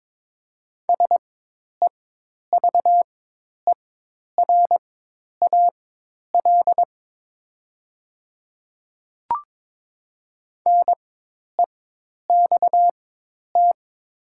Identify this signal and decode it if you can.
Morse code, heard as a keyed tone, about 22 wpm, spaced as for 10 wpm, 700 Hz. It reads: SEVERAL NEXT
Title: 4x Character Spacing - Top 500 Words - 22wpm